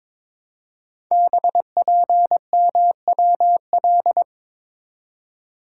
Morse code, heard as a keyed tone, about 22 wpm, 700 Hz. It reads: BPMWL
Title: Code Groups - 5 Letters - 22wpm